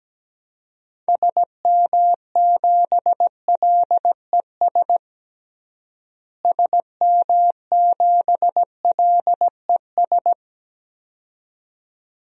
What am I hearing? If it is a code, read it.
Morse code, heard as a keyed tone, about 17 wpm, 700 Hz. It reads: SM7LES SM7LES
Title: DXCC-100 Call Signs - 6 Characters - Repeated 2x - 17wpm